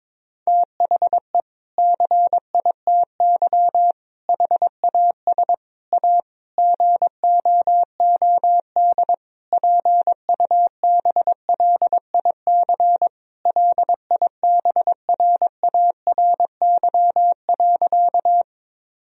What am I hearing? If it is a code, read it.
Morse code, heard as a keyed tone, about 22 wpm, 700 Hz. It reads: THE CITY HAS A GOOD PUBLIC LIBRARY.
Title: Sentences from Top 1000 Words - Review Part 4 - 22wpm